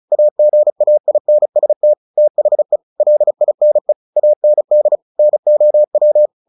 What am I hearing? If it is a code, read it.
Morse code, heard as a keyed tone, about 34 wpm, 600 Hz. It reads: AGAINST THE LINE AND NOW
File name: the old man and the sea 2 35wpm